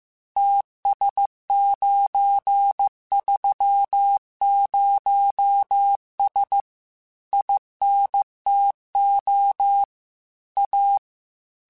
Morse code, heard as a keyed tone, about 15 wpm, 800 Hz. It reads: TS930S INTO A
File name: QSOs / QSOs-mp3s / WA7QOX 2@13